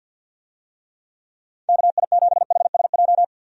Morse code, heard as a keyed tone, about 53 wpm, 700 Hz. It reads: KI7HSJ
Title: USA Call Signs - 6 Characters - 50wpm